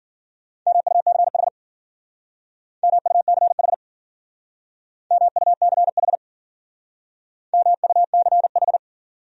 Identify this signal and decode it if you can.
Morse code, heard as a keyed tone, about 50 wpm, 700 Hz. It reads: MUCH MUCH MUCH MUCH